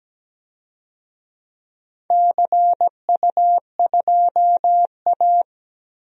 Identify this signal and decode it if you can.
Morse code, heard as a keyed tone, about 17 wpm, 700 Hz. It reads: CU2A
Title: DXCC-100 Call Signs - 4 Characters - 17wpm